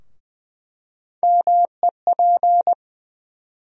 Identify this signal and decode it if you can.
Morse code, heard as a keyed tone, about 20 wpm, 700 Hz. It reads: MEP
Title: USA Call Sign Postfixes - 3 Letters 20wpm